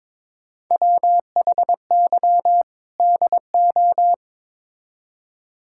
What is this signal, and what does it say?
Morse code, heard as a keyed tone, about 21 wpm, 700 Hz. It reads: WHY DO
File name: practice / 22wpm / Build A Phrase - 5-Word Phrases - 22wpm